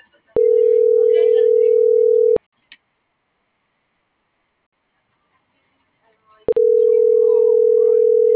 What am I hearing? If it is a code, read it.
Ringback tone (North America), 440 + 480 Hz.